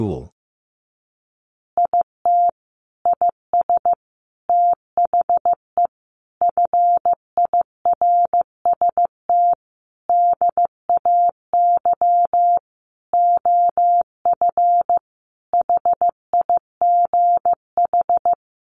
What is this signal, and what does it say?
Morse code, heard as a keyed tone, about 15 wpm, 700 Hz. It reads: IT IS THE FIRST DAY OF HIGH
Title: Additional Easy Phrases 15wpm